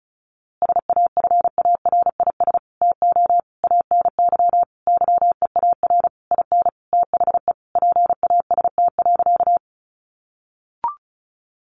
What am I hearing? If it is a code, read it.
Morse code, heard as a keyed tone, about 34 wpm, 700 Hz. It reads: SAFARIS TO ANY YEAR IN THE PAST.
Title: A Sound Of Thunder 35wpm